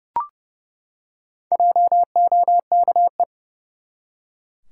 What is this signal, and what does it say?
Morse code, heard as a keyed tone, about 30 wpm, 700 Hz. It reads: JOKE
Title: Four Letter Words - 30wpm